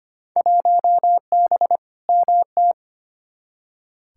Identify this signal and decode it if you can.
Morse code, heard as a keyed tone, about 25 wpm, 700 Hz. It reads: 1B MT
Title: Field Day Exchanges - 25wpm